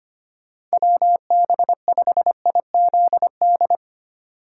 Morse code, heard as a keyed tone, about 25 wpm, 700 Hz. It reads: WB5IZD
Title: USA Call Signs - 4x Character Spacing - 25wpm